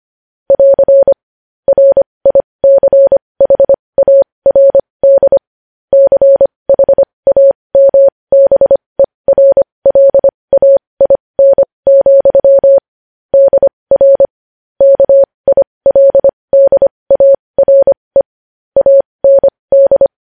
Morse code, heard as a keyed tone, about 25 wpm, spaced as for 20 wpm, 550 Hz. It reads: <AR> RICHARD CHAMBERLAIN, DR KILDARE AND
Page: news-headlines-in-morse-code-at-20-wpm